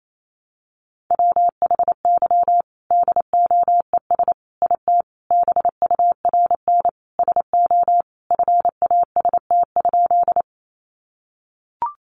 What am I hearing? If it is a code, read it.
Morse code, heard as a keyed tone, about 28 wpm, 700 Hz. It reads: WHY DOES IT BURN SO FAST?
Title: Sentences from Top 1300 Words 28wpm